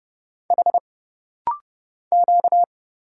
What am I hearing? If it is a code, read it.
Morse code, heard as a keyed tone, about 30 wpm, 700 Hz. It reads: H Q
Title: Single Letters - Rapid-Fire 30wpm